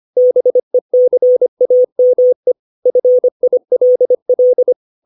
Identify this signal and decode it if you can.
Morse code, heard as a keyed tone, about 25 wpm, 500 Hz. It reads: BECAME FILL
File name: 632 words / 632 Words 25WPM